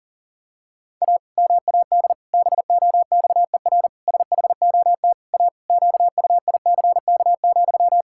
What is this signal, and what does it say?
Morse code, heard as a keyed tone, about 37 wpm, 700 Hz. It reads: A MAD BOXER SHOT A QUICK,
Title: Pangram Challenge v2 - 40wpm